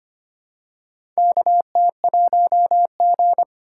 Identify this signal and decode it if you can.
Morse code, heard as a keyed tone, about 25 wpm, 700 Hz. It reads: KT1G
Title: USA Call Signs - 4 Characters - 3x Character Spacing - 25wpm